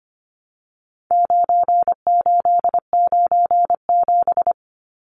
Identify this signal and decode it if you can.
Morse code, heard as a keyed tone, about 25 wpm, 700 Hz. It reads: 9897
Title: Numbers - 4-Digit 25wpm